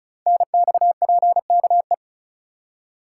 Morse code, heard as a keyed tone, about 36 wpm, 690 Hz. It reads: NXPKE
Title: Code Groups - 5 Letters - 35wpm